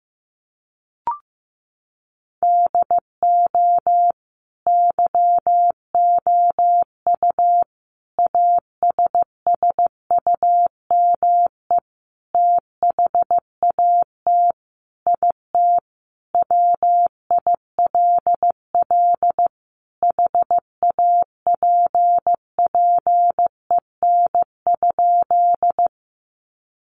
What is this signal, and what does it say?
Morse code, heard as a keyed tone, about 15 wpm, 700 Hz. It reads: DO YOU ASSUME THAT IT WILL HAPPEN?
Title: Sentences from Top 1500 Words 15wpm